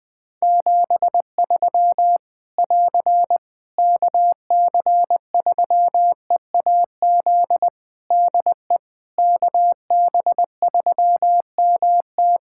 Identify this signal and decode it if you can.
Morse code, heard as a keyed tone, about 20 wpm, 700 Hz. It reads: 73 <AR> KC3EAZ DE KB3MT
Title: Practice QSO - Exchange 4 - Ending - 20wpm